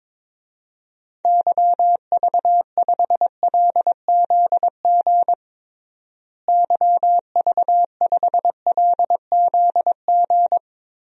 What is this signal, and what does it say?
Morse code, heard as a keyed tone, about 22 wpm, 700 Hz. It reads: YV5LZG YV5LZG
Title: DXCC-100 Call Signs - 6 Characters - Repeated 3x - 22wpm